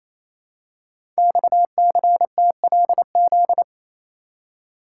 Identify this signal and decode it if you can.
Morse code, heard as a keyed tone, about 29 wpm, 700 Hz. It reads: XCTLZ
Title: Code Groups - 5 Letters - 28wpm